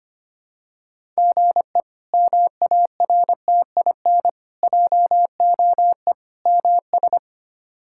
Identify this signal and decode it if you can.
Morse code, heard as a keyed tone, about 25 wpm, 700 Hz. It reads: GE MARTIN JOE MS
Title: K1USN SST QSO Practice - 25wpm